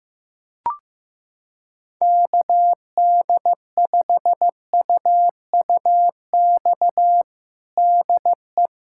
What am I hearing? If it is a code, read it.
Morse code, heard as a keyed tone, about 16 wpm, 700 Hz. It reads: KD5UUX DE
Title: USA Callsign DE Callsign 15wpm